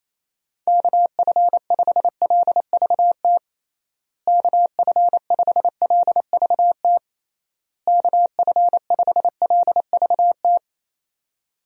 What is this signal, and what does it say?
Morse code, heard as a keyed tone, about 28 wpm, 700 Hz. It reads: KF5LVT KF5LVT KF5LVT
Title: USA Call Signs - 6 Characters - Repeated 3x - 28wpm